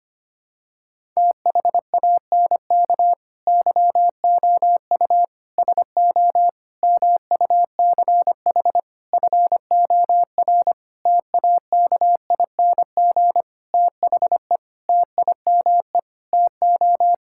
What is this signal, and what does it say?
Morse code, heard as a keyed tone, about 25 wpm, 700 Hz. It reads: THANK YOU SO MUCH FOR TAKING THE TIME TO
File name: practice / 25wpm / Sentences with 9-Word Phrases - 25wpm